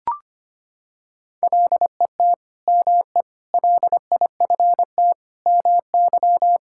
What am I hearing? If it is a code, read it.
Morse code, heard as a keyed tone, about 25 wpm, 700 Hz. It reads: LET ME LIFT MY
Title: Sentences from Top 1600 Words 25wpm